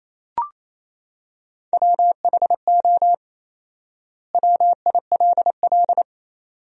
Morse code, heard as a keyed tone, about 28 wpm, 700 Hz. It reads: WHO WILL
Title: 4x Word-Spacing - Sentences from Top 200 Words - 28wpm